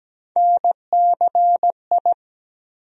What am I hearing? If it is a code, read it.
Morse code, heard as a keyed tone, about 17 wpm, 700 Hz. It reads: NCI
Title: USA Call Sign Postfixes - 3 Letters 17wpm